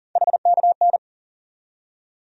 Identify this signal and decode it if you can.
Morse code, heard as a keyed tone, about 40 wpm, 700 Hz. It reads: HKN
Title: USA Call Sign Postfixes - 3 Letters 40wpm